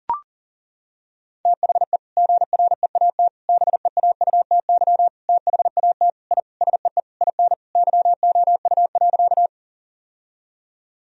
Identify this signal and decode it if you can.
Morse code, heard as a keyed tone, about 40 wpm, 700 Hz. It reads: THE GREAT BEAUTY THAT I SEE IN YOU.
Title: Sentences from Top 1000 Words - Review Part 2 - 40wpm